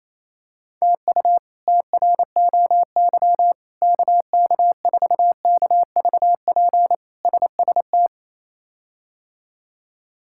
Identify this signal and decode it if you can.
Morse code, heard as a keyed tone, about 28 wpm, 700 Hz. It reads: TU TROY KK4KVP SST
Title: K1USN SST QSO Practice - 28wpm